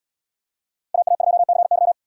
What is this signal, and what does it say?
Morse code, heard as a keyed tone, about 77 wpm, 700 Hz. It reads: AA2CY